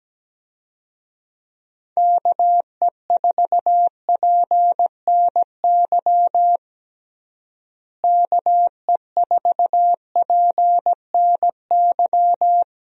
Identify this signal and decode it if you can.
Morse code, heard as a keyed tone, about 17 wpm, 700 Hz. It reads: KE4PNY KE4PNY
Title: USA Call Signs - 6 Characters - Repeated 3x - 17wpm